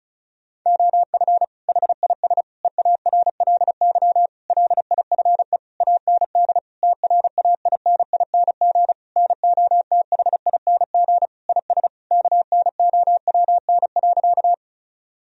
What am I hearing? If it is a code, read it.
Morse code, heard as a keyed tone, about 36 wpm, 700 Hz. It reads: OF HIS EARLY LIFE AND TRAINING NOTHING IS KNOWN.
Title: Sentences from Top 1000 Words - Review Part 4 - 35wpm